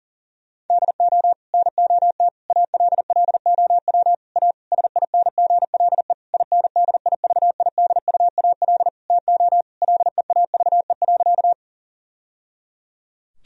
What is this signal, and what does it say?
Morse code, heard as a keyed tone, about 40 wpm, 700 Hz. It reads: DO NOT ALLOW A SINGLE INDIVIDUAL TO LEAVE.
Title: Sentences from Top 1000 Words - Review Part 4 - 40wpm